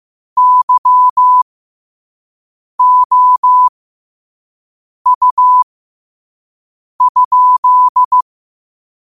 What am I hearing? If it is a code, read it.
Morse code, heard as a keyed tone, about 15 wpm, spaced as for 5 wpm, 1000 Hz. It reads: YOU?
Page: Quotes for Sat, 16 Aug 2025 in Morse Code at 5 words per minute.